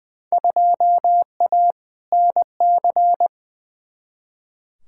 Morse code, heard as a keyed tone, about 20 wpm, 700 Hz. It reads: 2A NC
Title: Field Day Exchanges Encore - 20wpm